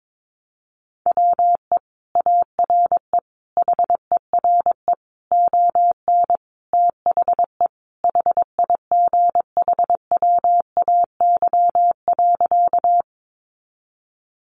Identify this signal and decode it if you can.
Morse code, heard as a keyed tone, about 22 wpm, 700 Hz. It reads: WE ARE HERE ON THE HIGHWAY.
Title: Sentences from Top 1400 Words 22wpm